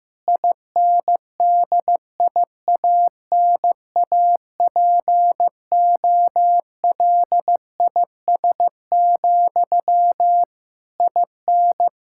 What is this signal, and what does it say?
Morse code, heard as a keyed tone, about 16 wpm, 700 Hz. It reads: INDIANAPOLIS, IN
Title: Practice QSO - Exchange 1 - Beginning - 15wpm